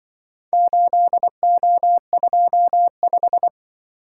Morse code, heard as a keyed tone, about 25 wpm, 700 Hz. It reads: 8O25